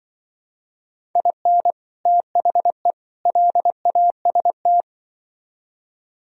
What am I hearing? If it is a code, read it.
Morse code, heard as a keyed tone, about 24 wpm, 700 Hz. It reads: IN THE LAST
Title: Speed-Racing - Sets of 3 Words 20wpm